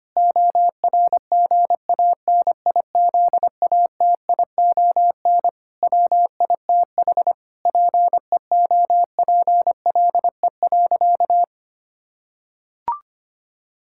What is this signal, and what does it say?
Morse code, heard as a keyed tone, about 25 wpm, 700 Hz. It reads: ORGANIZATION WITH PEOPLE.
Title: Sentences from Top 1000 Words - Review Part 5 - 25wpm